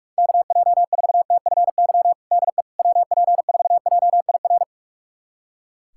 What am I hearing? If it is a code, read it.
Morse code, heard as a keyed tone, about 47 wpm, 700 Hz. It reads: KJ4TFY DE WP4JIR
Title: USA Callsign DE Callsign 45wpm